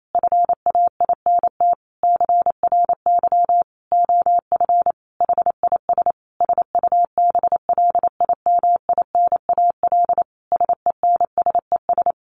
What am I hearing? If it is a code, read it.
Morse code, heard as a keyed tone, about 28 wpm, 700 Hz. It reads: FAINT CRY OF HIS SUBLIMINAL SENSES
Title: A Sound Of Thunder 28wpm